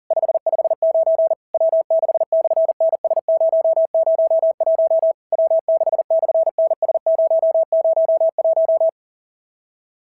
Morse code, heard as a keyed tone, about 43 wpm, 650 Hz. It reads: 559 W6/NS001 W6/NS001
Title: SOTA QSO Practice 40wpm